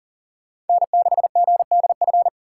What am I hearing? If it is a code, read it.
Morse code, heard as a keyed tone, about 52 wpm, 700 Hz. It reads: N6GDF